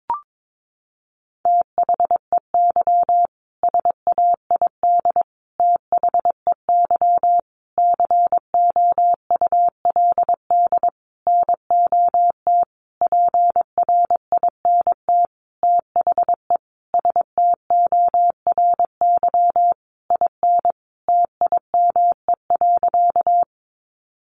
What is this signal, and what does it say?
Morse code, heard as a keyed tone, about 22 wpm, 700 Hz. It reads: THEY SAID THEY COULD NOT PRINT THE STORY IN TIME.
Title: Sentences from Top 1400 Words 22wpm